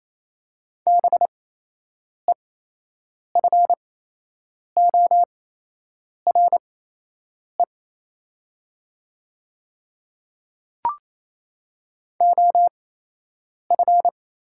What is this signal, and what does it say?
Morse code, heard as a keyed tone, about 28 wpm, spaced as for 8 wpm, 700 Hz. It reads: BEFORE OF
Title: 8x Character Spacing - Top 300 Words - 28wpm